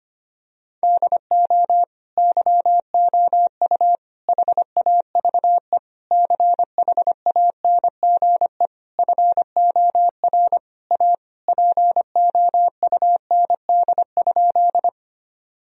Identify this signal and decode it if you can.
Morse code, heard as a keyed tone, about 25 wpm, 700 Hz. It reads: DO YOU HAVE CHANGE FOR A POUND?
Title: Sentences from Top 1000 Words - Review Part 2 - 25wpm